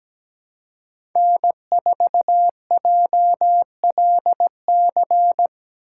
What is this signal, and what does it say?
Morse code, heard as a keyed tone, about 17 wpm, 700 Hz. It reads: N4JLC
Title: USA Call Signs - 5 Characters - 3x Character Spacing - 17wpm